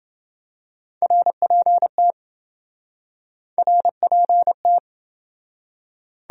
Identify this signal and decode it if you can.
Morse code, heard as a keyed tone, about 28 wpm, 700 Hz. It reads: RPT RPT